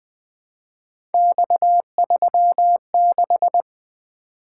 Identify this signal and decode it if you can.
Morse code, heard as a keyed tone, about 20 wpm, 700 Hz. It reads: X36
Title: Three Letter-Number Combinations - 20wpm